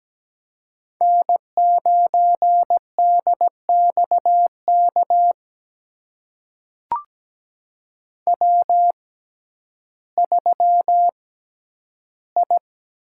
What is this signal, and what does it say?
Morse code, heard as a keyed tone, about 17 wpm, spaced as for 10 wpm, 700 Hz. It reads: N9DXK W3I
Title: USA Call Signs - 5 Characters - 6x Character Spacing - 17wpm